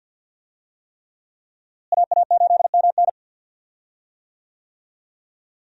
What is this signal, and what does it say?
Morse code, heard as a keyed tone, about 47 wpm, 700 Hz. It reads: AA8MN
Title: USA Call Signs - 5 Characters - Repeated 3x - 50wpm